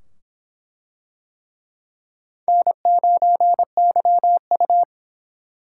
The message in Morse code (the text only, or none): N9YU